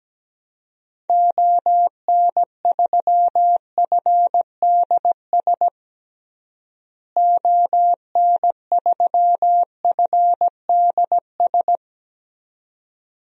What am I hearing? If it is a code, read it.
Morse code, heard as a keyed tone, about 17 wpm, 700 Hz. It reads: ON3FDS ON3FDS
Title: DXCC-100 Call Signs - 6 Characters - Repeated 2x - 17wpm